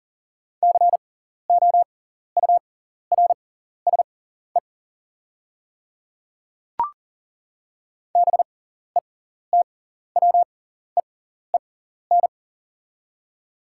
Morse code, heard as a keyed tone, about 40 wpm, spaced as for 14 wpm, 690 Hz. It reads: COURSE BETWEEN
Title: 6x Character Spacing - Top 500 Words - 40wpm